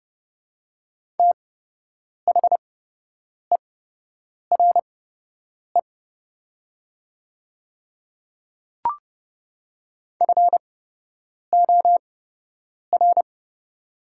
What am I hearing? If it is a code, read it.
Morse code, heard as a keyed tone, about 30 wpm, spaced as for 8 wpm, 700 Hz. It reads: THERE FOR
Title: 8x Character Spacing - Top 100 Words - 30wpm